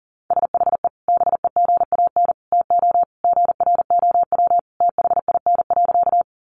Morse code, heard as a keyed tone, about 37 wpm, 700 Hz. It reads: SHE BEGAN TO GROW THIN.
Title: Sentences from Top 1200 Words 40wpm